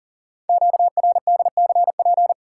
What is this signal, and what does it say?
Morse code, heard as a keyed tone, about 42 wpm, 700 Hz. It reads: QRDCP
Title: Code Groups - 5 Letter:Numbers - 40wpm